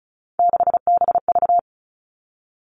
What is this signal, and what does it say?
Morse code, heard as a keyed tone, about 36 wpm, 700 Hz. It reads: 6BV
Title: Three Challenging Letter-Number Combinations 35wpm